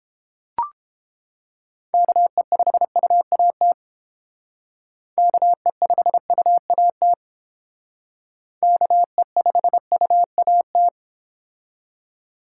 Morse code, heard as a keyed tone, about 32 wpm, 700 Hz. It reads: KE5UAT KE5UAT KE5UAT